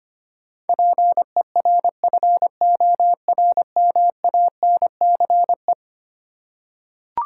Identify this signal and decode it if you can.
Morse code, heard as a keyed tone, about 25 wpm, 700 Hz. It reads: PERFORMANCE
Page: morse-code-ninja-practice-25wpm